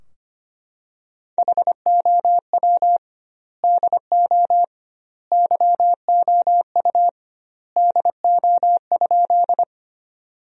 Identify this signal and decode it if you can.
Morse code, heard as a keyed tone, about 25 wpm, 700 Hz. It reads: HOW DO YOU DO?
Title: 2x Word-Spacing - Sentences from Top 100 Words - 25wpm